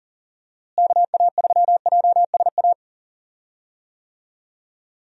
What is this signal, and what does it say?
Morse code, heard as a keyed tone, about 40 wpm, 700 Hz. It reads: KA3JSA
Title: USA Call Signs - 6 Characters - Repeated 3x - 40wpm